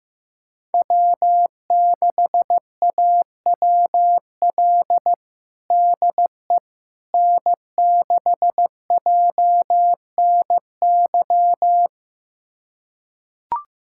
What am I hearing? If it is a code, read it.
Morse code, heard as a keyed tone, about 15 wpm, 700 Hz. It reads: W6AWL DE N6JNY
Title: USA Callsign DE Callsign - 5 Characters - 15wpm